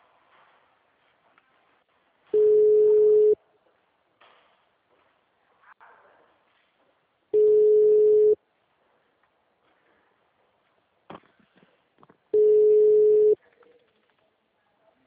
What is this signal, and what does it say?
Ringback tone (Europe, 425 Hz).